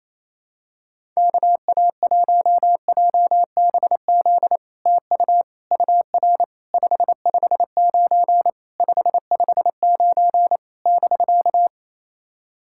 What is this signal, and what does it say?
Morse code, heard as a keyed tone, about 28 wpm, 700 Hz. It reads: KA1JBZ TU UR 559 559 <BK>
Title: SOTA QSO Practice 28wpm